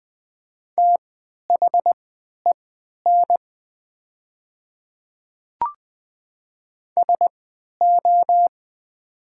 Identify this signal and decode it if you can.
Morse code, heard as a keyed tone, about 20 wpm, spaced as for 11 wpm, 700 Hz. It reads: THEN SO
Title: 3x Character Spacing - Top 100 Words - 20wpm